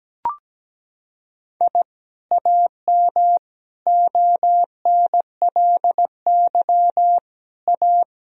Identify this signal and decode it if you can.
Morse code, heard as a keyed tone, about 17 wpm, 700 Hz. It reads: I AM ONLY A
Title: Sentences from Top 1000 Words - Review Part 2 - 17wpm